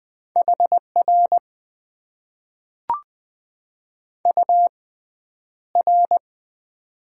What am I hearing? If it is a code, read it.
Morse code, heard as a keyed tone, about 20 wpm, spaced as for 10 wpm, 700 Hz. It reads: HR UR
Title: Top 90 QSO Elements - 6x Character Spacing - 20wpm